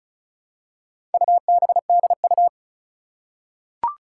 Morse code, heard as a keyed tone, about 36 wpm, 700 Hz. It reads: UBDU
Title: Four Challenging Letter-Number Combinations 35wpm